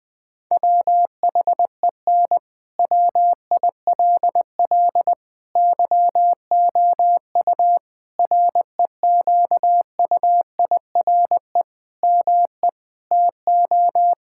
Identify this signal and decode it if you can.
Morse code, heard as a keyed tone, about 19 wpm, 700 Hz. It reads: WHEN WILL YOU REQUIRE ME TO
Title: Sentences from Top 1000 Words - Review Part 6 - 20wpm